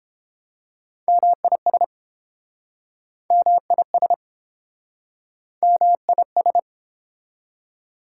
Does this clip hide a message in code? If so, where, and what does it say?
Yes: Morse code from 1.1 to 6.6 s, heard as a keyed tone, about 29 wpm, 700 Hz. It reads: MIS MIS MIS